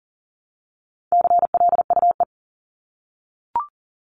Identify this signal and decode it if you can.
Morse code, heard as a keyed tone, about 40 wpm, 700 Hz. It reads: CLUE
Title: Top 1751-2000 Words 40wpm